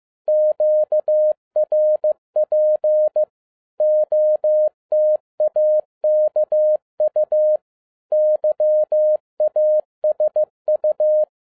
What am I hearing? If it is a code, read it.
Morse code, heard as a keyed tone, about 15 wpm, 610 Hz. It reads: QRP OTAKU YASU
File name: 2011-07-mp3 / n22 QRP OTAKU 15wpm